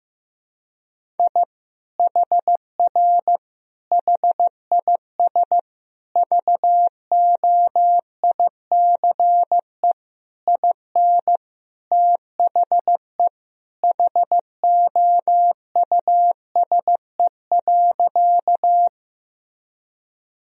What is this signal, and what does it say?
Morse code, heard as a keyed tone, about 15 wpm, 700 Hz. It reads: I HR HIS VOICE IN THE HOUSE.
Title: Sentences from Top 500 Words and Abbreviations 15wpm